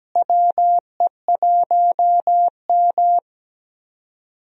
Morse code, heard as a keyed tone, about 17 wpm, 700 Hz. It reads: WE1M
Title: USA Call Signs - 4 Characters - Repeated 3x - 17wpm